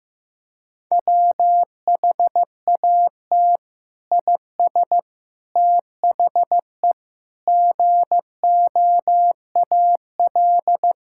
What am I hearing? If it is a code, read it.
Morse code, heard as a keyed tone, about 15 wpm, 700 Hz. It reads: WHAT IS THE GOAL
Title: Sentences from Top 1000 Words - Review Part 5 - 15wpm